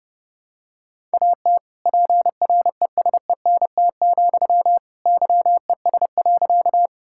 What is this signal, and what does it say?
Morse code, heard as a keyed tone, about 30 wpm, 700 Hz. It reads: AT PRESENT, YES.
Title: Sentences from Top 1000 Words - Review Part 5 - 30wpm